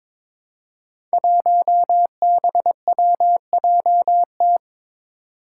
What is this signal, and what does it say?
Morse code, heard as a keyed tone, about 23 wpm, 700 Hz. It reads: 1BWJT
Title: Five Challenging Letter-Number Combinations 22wpm